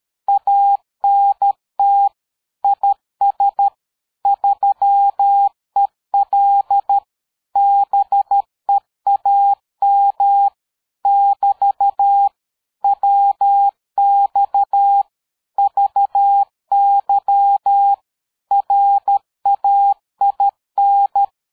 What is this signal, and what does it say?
Morse code, heard as a keyed tone, about 13 wpm, 800 Hz. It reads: ANT IS 3EL BEAM <BT> WX VY RAIN
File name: dk5ke / QSO 60BpM